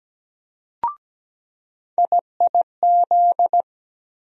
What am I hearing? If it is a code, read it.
Morse code, heard as a keyed tone, about 17 wpm, 700 Hz. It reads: IIZ
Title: USA Call Sign Postfixes - 3 Letters 17wpm